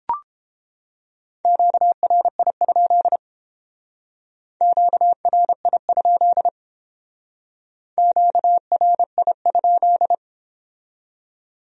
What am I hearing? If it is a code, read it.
Morse code, heard as a keyed tone, about 30 wpm, 700 Hz. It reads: QRI? QRI? QRI?